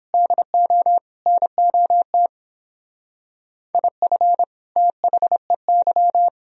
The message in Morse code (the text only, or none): DO NOT IF THEY